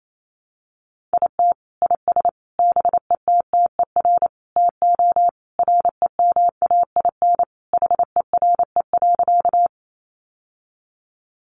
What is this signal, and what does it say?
Morse code, heard as a keyed tone, about 28 wpm, 700 Hz. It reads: IT IS BETTER TO REMAIN HERE.
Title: Sentences from Top 1200 Words 28wpm